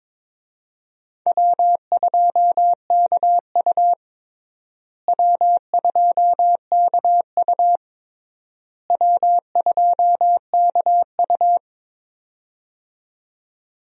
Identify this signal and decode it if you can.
Morse code, heard as a keyed tone, about 22 wpm, 700 Hz. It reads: W2KU W2KU W2KU
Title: USA Call Signs - 4 Characters - Repeated 3x - 22wpm